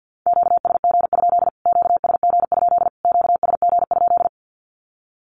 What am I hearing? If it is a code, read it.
Morse code, heard as a keyed tone, about 48 wpm, 700 Hz. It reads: QSG? QSG? QSG?
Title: Q-Codes Repeated 3x 50wpm